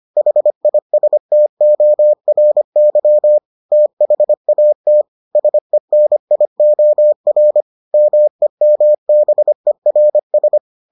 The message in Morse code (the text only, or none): HISTORY THAT SENIOR MEMBERS